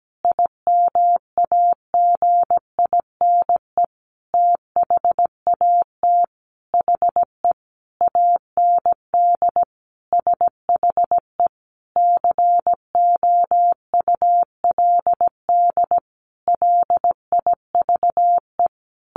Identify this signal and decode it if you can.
Morse code, heard as a keyed tone, about 17 wpm, 700 Hz. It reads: IMAGINE THAT HE AND SHE COULD LIVE
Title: Sentences from Top 1400 Words 17wpm